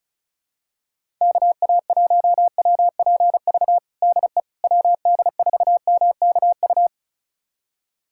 Morse code, heard as a keyed tone, about 36 wpm, 700 Hz. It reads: KA1WPV DE WD4MKU
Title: USA Callsign DE Callsign - 6 Characters - 35wpm